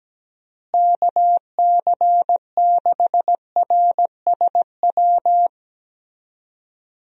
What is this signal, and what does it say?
Morse code, heard as a keyed tone, about 17 wpm, 700 Hz. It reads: KC6RSW
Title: USA Call Signs - 6 Characters - Repeated 3x - 17wpm